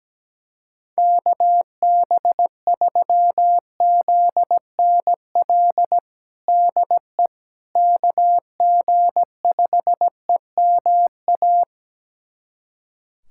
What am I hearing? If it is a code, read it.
Morse code, heard as a keyed tone, about 17 wpm, 700 Hz. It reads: KB3ZNL DE KG5EMA
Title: USA Callsign DE Callsign - 6 Characters - 17wpm